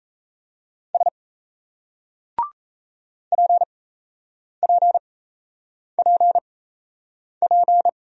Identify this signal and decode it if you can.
Morse code, heard as a keyed tone, about 38 wpm, spaced as for 8 wpm, 700 Hz. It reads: S PPPP